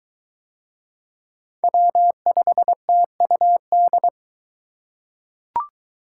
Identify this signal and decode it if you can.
Morse code, heard as a keyed tone, about 22 wpm, 700 Hz. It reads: W5TUD